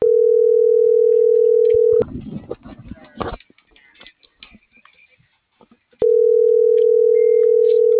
Ringback tone (North America), 440 + 480 Hz.